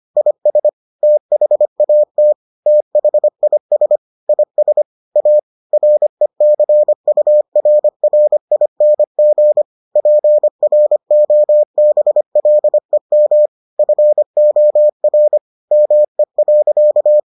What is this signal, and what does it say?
Morse code, heard as a keyed tone, about 25 wpm, 600 Hz. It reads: IS THAT THIS IS A RECURRING PROBLEM FOR ME.